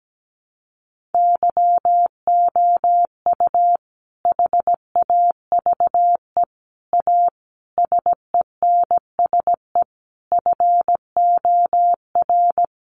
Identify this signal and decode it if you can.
Morse code, heard as a keyed tone, about 17 wpm, 700 Hz. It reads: YOU HAVE A SENSE FOR
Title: Sentences from Top 700 Words 17wpm